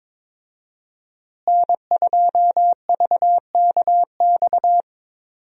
Morse code, heard as a keyed tone, about 21 wpm, 700 Hz. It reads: N2VKX
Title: USA Call Signs - 5 Characters - Repeated 2x - 22wpm